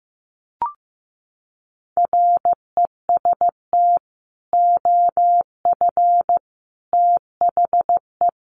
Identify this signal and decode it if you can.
Morse code, heard as a keyed tone, about 15 wpm, 700 Hz. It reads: REST OF THE
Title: Sets of 4 Words 15wpm